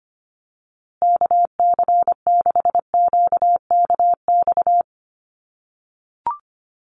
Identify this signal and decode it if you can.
Morse code, heard as a keyed tone, about 25 wpm, 700 Hz. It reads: KC6QKX
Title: USA Call Signs Repeated 3x 25wpm